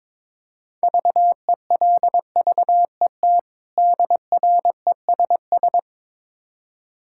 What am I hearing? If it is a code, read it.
Morse code, heard as a keyed tone, about 22 wpm, 700 Hz. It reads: VELVET DRESS
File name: Difficult Words with Hints - 22wpm